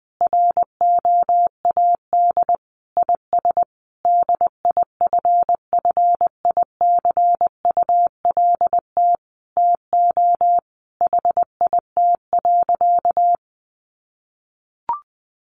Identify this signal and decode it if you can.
Morse code, heard as a keyed tone, about 20 wpm, 700 Hz. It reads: ROAD IS DIFFICULT TO HIT.
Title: Sentences from Top 1200 Words 20wpm